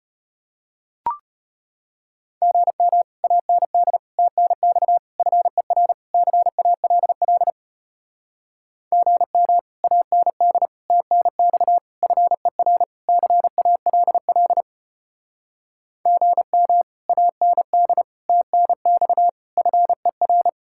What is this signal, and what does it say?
Morse code, heard as a keyed tone, about 33 wpm, 700 Hz. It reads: GM AND TNX FER CALL GM AND TNX FER CALL GM AND TNX FER